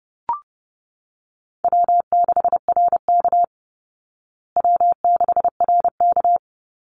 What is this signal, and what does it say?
Morse code, heard as a keyed tone, about 30 wpm, 700 Hz. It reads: W6RK W6RK
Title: Easy Call Signs Repeated 2x - 30wpm